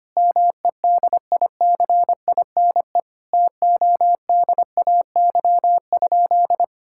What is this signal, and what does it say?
Morse code, heard as a keyed tone, about 25 wpm, 700 Hz. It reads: MEDICINE TODAY?
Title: Sentences from Top 1000 Words - Review Part 6 - 25wpm